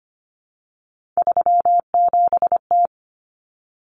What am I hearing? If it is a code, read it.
Morse code, heard as a keyed tone, about 25 wpm, 700 Hz. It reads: 37T
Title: Common Cut Numbers - 3-Digit 25wpm